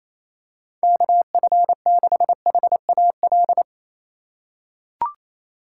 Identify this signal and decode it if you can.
Morse code, heard as a keyed tone, about 28 wpm, 700 Hz. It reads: KF6HAL
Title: USA Call Signs - 4x Character Spacing - 28wpm